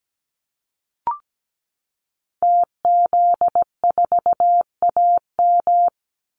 Morse code, heard as a keyed tone, about 17 wpm, 700 Hz. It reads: TZ4AM
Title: Call Signs - 17wpm